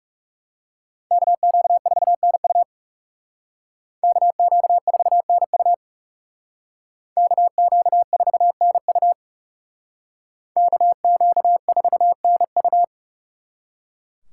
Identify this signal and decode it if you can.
Morse code, heard as a keyed tone, about 35 wpm, 700 Hz. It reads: KQ4NU KQ4NU KQ4NU KQ4NU
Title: Speed Racing - USA Call Signs 30wpm